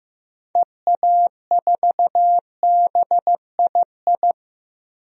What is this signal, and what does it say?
Morse code, heard as a keyed tone, about 15 wpm, 700 Hz. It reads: EA4BII
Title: DXCC-100 Call Signs - 6 Characters - 15wpm